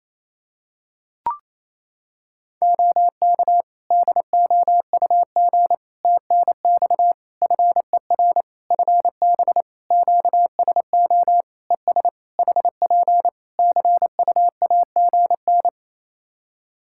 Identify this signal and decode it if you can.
Morse code, heard as a keyed tone, about 28 wpm, 700 Hz. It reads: OK DOUG TNX FER FB QSO ES HP CUAGN
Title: Practice QSO - Exchange 4 - Ending - 28wpm